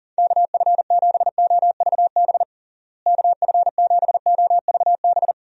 Morse code, heard as a keyed tone, about 40 wpm, 700 Hz. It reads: KF7OVB KF7OVB
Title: USA Call Signs - 6 Characters - Repeated 2x - 40wpm